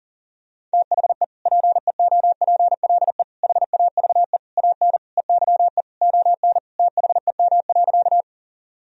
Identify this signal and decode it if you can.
Morse code, heard as a keyed tone, about 40 wpm, 700 Hz. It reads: THE PEOPLE HAVE AN EYE ON THEM.
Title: Sentences from Top 1000 Words - Review Part 1 - 40wpm